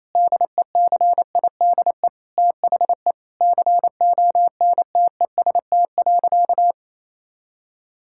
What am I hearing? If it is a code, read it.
Morse code, heard as a keyed tone, about 28 wpm, 700 Hz. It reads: DECIDE THE CONTEST.
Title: Sentences from Top 1600 Words 28wpm